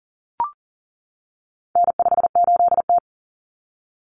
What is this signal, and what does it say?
Morse code, heard as a keyed tone, about 40 wpm, 700 Hz. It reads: N58T
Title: Common Cut Numbers - 4-Digit 40wpm